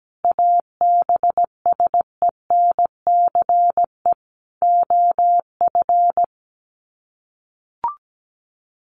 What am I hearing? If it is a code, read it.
Morse code, heard as a keyed tone, about 17 wpm, 700 Hz. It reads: ABSENCE OF
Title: Sets of 4 Words 17wpm